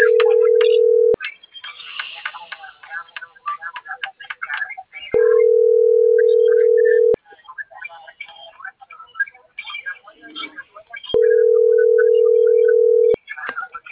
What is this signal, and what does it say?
Ringback tone (North America), 440 + 480 Hz.